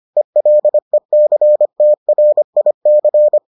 Morse code, heard as a keyed tone, about 25 wpm, 600 Hz. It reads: ELECTRIC